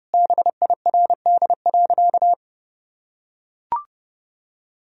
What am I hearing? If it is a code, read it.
Morse code, heard as a keyed tone, about 32 wpm, 700 Hz. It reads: BIRD.
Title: Sentences from Top 1000 Words - Review Part 2 - 30wpm